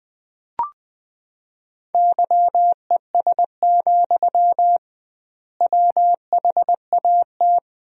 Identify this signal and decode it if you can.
Morse code, heard as a keyed tone, about 21 wpm, 700 Hz. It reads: YES, WHAT
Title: 2x Word-Spacing - Sentences from Top 500 Words - 20wpm